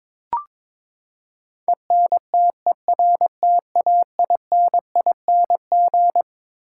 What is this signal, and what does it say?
Morse code, heard as a keyed tone, about 22 wpm, 700 Hz. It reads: ENTERTAINING
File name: Twelve Letter Words - 22wpm